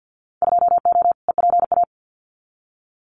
Morse code, heard as a keyed tone, about 50 wpm, 700 Hz. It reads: 2O EPA
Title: Winter Field Day Exchanges Encore 50wpm